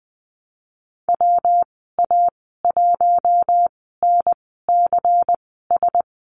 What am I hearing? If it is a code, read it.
Morse code, heard as a keyed tone, about 20 wpm, spaced as for 15 wpm, 700 Hz. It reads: WA1NCS
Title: 122 Lesson 20g - Number 3 - Call Signs 20wpm